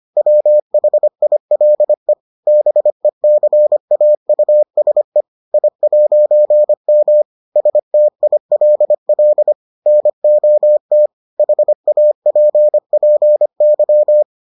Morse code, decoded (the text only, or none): WHILE BECAUSE I'M STILL NOT HAPPY